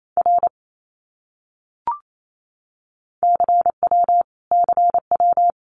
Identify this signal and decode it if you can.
Morse code, heard as a keyed tone, about 28 wpm, 700 Hz. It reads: R CW CW
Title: Abbreviations Repeated 3x - 28wpm